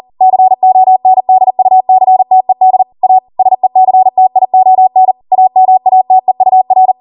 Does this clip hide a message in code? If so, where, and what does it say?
Yes: Morse code from 0.2 to 6.9 s, heard as a keyed tone, about 40 wpm, 750 Hz. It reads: CONDUCTED A SECTION AMATEUR